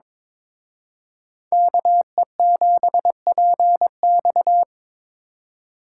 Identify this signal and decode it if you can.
Morse code, heard as a keyed tone, about 26 wpm, 700 Hz. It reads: KE7PX
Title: Speed Racing - USA Call Signs 22wpm